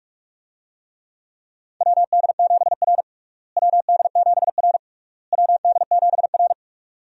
Morse code, heard as a keyed tone, about 46 wpm, 700 Hz. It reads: WD7R WD7R WD7R
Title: USA Call Signs - 4 Characters - Repeated 3x - 45wpm